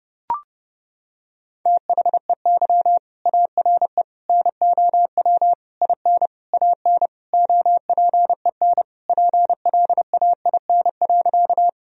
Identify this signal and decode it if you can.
Morse code, heard as a keyed tone, about 30 wpm, 700 Hz. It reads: THEY ARE NOW IN AN OPEN PLAIN.
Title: Sentences from Top 1000 Words - Review Part 2 - 30wpm